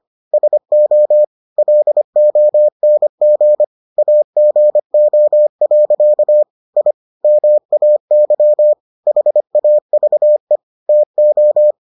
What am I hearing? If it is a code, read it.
Morse code, heard as a keyed tone, about 25 wpm, 600 Hz. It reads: SO LONG AGO. I MAY HAVE TO